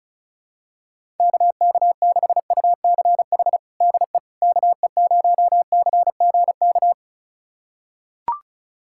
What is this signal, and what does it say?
Morse code, heard as a keyed tone, about 34 wpm, 700 Hz. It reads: KK6UCH DE KE0CGK
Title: USA Callsign DE Callsign 35wpm